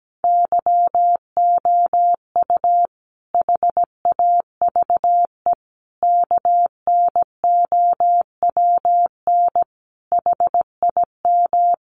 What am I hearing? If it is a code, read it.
Morse code, heard as a keyed tone, about 17 wpm, 700 Hz. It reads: YOU HAVE KNOWN HIM
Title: Sentences from Top 600 Words 17wpm